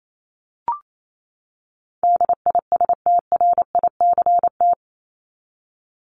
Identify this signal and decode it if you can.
Morse code, heard as a keyed tone, about 28 wpm, 700 Hz. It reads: DISTRICT
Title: Top 1501-1750 Words 28wpm